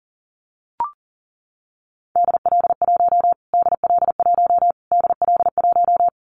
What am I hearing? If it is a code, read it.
Morse code, heard as a keyed tone, about 40 wpm, 700 Hz. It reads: DL1 DL1 DL1
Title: Callsign Prefixes for Top 25 DXCC - Repeated 3x - 40wpm